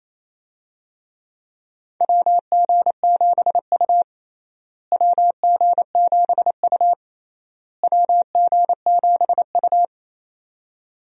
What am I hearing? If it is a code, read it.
Morse code, heard as a keyed tone, about 27 wpm, 700 Hz. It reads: WG7U WG7U WG7U
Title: USA Call Signs - 4 Characters - Repeated 3x - 28wpm